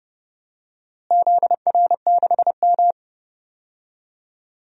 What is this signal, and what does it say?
Morse code, heard as a keyed tone, about 32 wpm, 700 Hz. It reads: ZR6M
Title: DXCC-100 Call Signs - 4 Characters - Repeated 2x - 30wpm